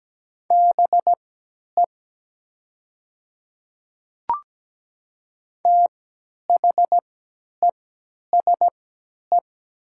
Morse code, heard as a keyed tone, about 17 wpm, spaced as for 10 wpm, 700 Hz. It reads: BE THESE
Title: 3x Character Spacing - Top 300 Words - 17wpm